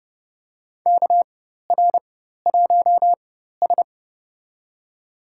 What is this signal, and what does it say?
Morse code, heard as a keyed tone, about 30 wpm, spaced as for 14 wpm, 700 Hz. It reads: KR1S
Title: USA Call Signs - 4 Characters - 4x Character Spacing - 30wpm